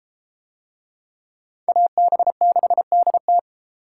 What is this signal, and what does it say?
Morse code, heard as a keyed tone, about 30 wpm, 700 Hz. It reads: AB6DT